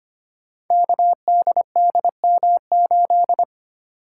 Morse code, heard as a keyed tone, about 25 wpm, 700 Hz. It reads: KDDM8
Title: Code Groups - 5 Letter:Numbers - 25wpm